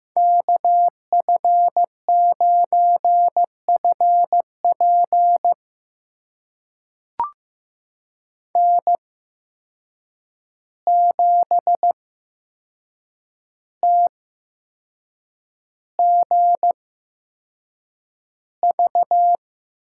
Morse code, heard as a keyed tone, about 15 wpm, spaced as for 6 wpm, 700 Hz. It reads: KF9FP N7TGV
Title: USA Call Signs - 5 Characters - 8x Character Spacing - 15wpm